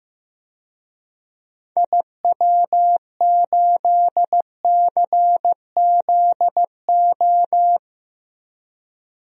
Morse code, heard as a keyed tone, about 15 wpm, 700 Hz. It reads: IW8CZO
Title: DXCC-100 Call Signs - 6 Characters - Repeated 2x - 15wpm